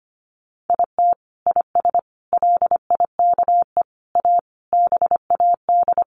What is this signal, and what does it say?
Morse code, heard as a keyed tone, about 25 wpm, 700 Hz. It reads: IT IS LIKE A BAD
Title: Sentences from Top 1500 Words 25wpm